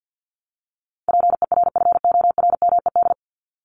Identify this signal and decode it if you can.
Morse code, heard as a keyed tone, about 50 wpm, 700 Hz. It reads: PERFORMED
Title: ED Suffix - Top 100 Words with Base Words 50wpm